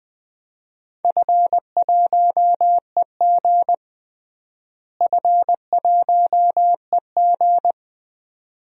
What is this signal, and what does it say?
Morse code, heard as a keyed tone, about 20 wpm, 700 Hz. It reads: F1EG F1EG
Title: DXCC-100 Call Signs - 4 Characters - Repeated 2x - 20wpm